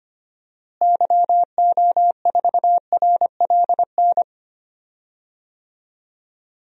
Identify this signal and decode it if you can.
Morse code, heard as a keyed tone, about 25 wpm, 700 Hz. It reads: YO4RLN
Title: DXCC-100 Call Signs - 6 Characters - Repeated 2x - 25wpm